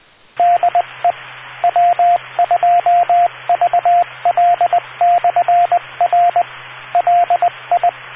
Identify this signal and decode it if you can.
Morse code, heard as a keyed tone, about 20 wpm, 700 Hz, with radio noise. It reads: DE W2VL/R LI